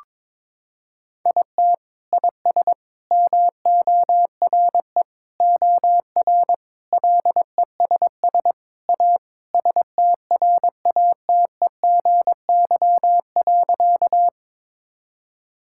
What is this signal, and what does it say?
Morse code, heard as a keyed tone, about 22 wpm, 700 Hz. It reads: IT IS MORE OR LESS A STRATEGY.
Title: Sentences from Top 1000 Words - Review Part 5 - 22wpm